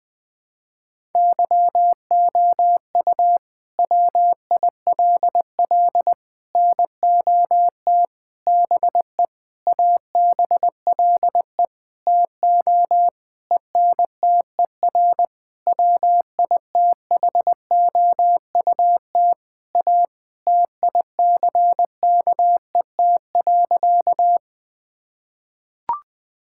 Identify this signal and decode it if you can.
Morse code, heard as a keyed tone, about 20 wpm, 700 Hz. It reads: YOU WILL NOT BE ABLE TO ENTER WITHOUT A TICKET.
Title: Sentences with 6-Word Phrases - 20wpm